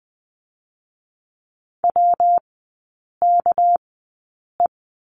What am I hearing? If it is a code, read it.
Morse code, heard as a keyed tone, about 20 wpm, 700 Hz. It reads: W K E
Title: 198 Lesson 32e - Letter k - Three Characters 20wpm